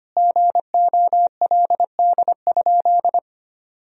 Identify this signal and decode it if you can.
Morse code, heard as a keyed tone, about 25 wpm, 700 Hz. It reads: GOLD?
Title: Sentences from Top 1000 Words - Review Part 3 - 25wpm